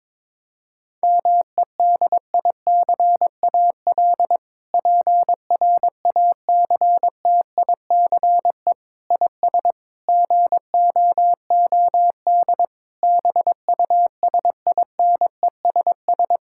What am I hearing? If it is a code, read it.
Morse code, heard as a keyed tone, about 22 wpm, 700 Hz. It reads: MEDICAL PRACTICE IS GOOD BUSINESS
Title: Sentences from Top 1000 Words - Review Part 5 - 22wpm